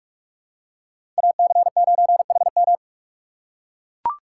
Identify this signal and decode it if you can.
Morse code, heard as a keyed tone, about 42 wpm, 700 Hz. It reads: AC9HM
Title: USA Call Signs - 5 Characters - 3x Character Spacing - 45wpm